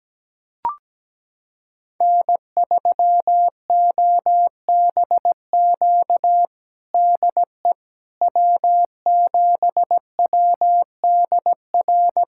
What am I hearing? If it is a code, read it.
Morse code, heard as a keyed tone, about 17 wpm, 700 Hz. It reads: N3OBQ DE W7WDR
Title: USA Callsign DE Callsign - 5 Characters - 17wpm